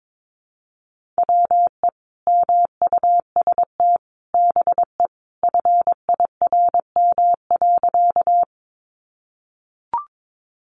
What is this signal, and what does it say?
Morse code, heard as a keyed tone, about 22 wpm, 700 Hz. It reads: WE MUST BE FIRM.
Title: Sentences from Top 1400 Words 22wpm